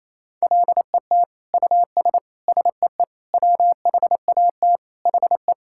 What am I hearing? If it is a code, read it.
Morse code, heard as a keyed tone, about 27 wpm, 700 Hz. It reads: LET US SEE WHAT HE
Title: Sentences from Top 1000 Words - Review Part 5 - 28wpm